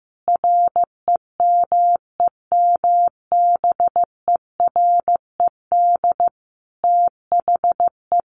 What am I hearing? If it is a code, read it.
Morse code, heard as a keyed tone, about 15 wpm, 700 Hz. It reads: REMEMBERED THE
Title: A Sound Of Thunder 15wpm